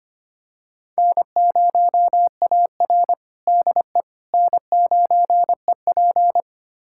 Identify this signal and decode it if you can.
Morse code, heard as a keyed tone, about 25 wpm, 700 Hz. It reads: N0AR DE N9EP
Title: USA Callsign DE Callsign - 4 Characters - 25wpm